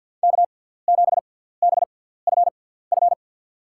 Morse code, heard as a keyed tone, about 52 wpm, spaced as for 17 wpm, 700 Hz. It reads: K7BFF
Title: USA Call Signs - 5 Characters - 6x Character Spacing - 50wpm